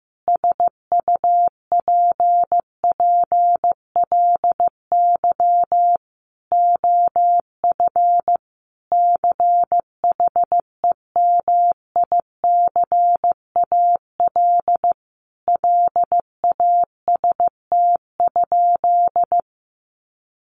Morse code, heard as a keyed tone, about 15 wpm, 700 Hz. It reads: SUPPLY OF CHEMICAL LAST?
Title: Sentences from Top 1200 Words 15wpm